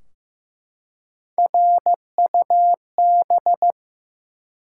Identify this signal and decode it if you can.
Morse code, heard as a keyed tone, about 15 wpm, 700 Hz. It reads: RUB
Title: Three Letter Words - 15wpm